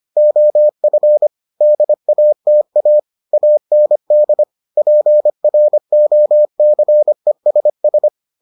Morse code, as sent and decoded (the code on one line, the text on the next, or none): --- ..-. / -.. .- - .- / .- -. -.. / .--. .-. --- -.-. ... ...
OF DATA AND PROCESS